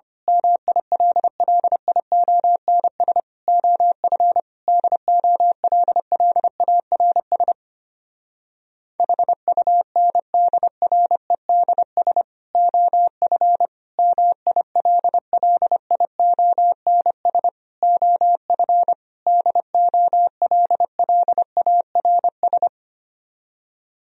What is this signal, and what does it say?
Morse code, heard as a keyed tone, about 26 wpm, 700 Hz. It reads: MILLIONS OF DOLLARS HUNDREDS OF MILLIONS OF DOLLARS
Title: Speed-Racing - Sets of 5 Words 25wpm